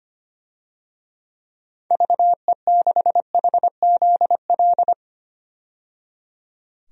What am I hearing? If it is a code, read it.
Morse code, heard as a keyed tone, about 25 wpm, 700 Hz. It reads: VE6HZL
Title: DXCC-100 Call Signs - 6 Characters - 25wpm